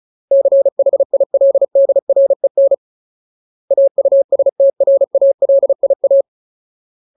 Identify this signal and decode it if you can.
Morse code, heard as a keyed tone, about 34 wpm, 550 Hz. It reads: CHILDREN AUSTRALIA
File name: Ultimate Word List / 35WPM Ultimate